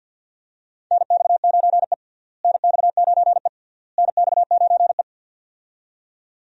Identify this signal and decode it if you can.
Morse code, heard as a keyed tone, about 50 wpm, 700 Hz. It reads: NX9E NX9E NX9E
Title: USA Call Signs - 4 Characters - Repeated 3x - 50wpm